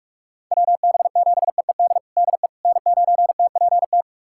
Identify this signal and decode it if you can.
Morse code, heard as a keyed tone, about 46 wpm, 700 Hz. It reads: WB7EED DE N9TPT
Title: USA Callsign DE Callsign 45wpm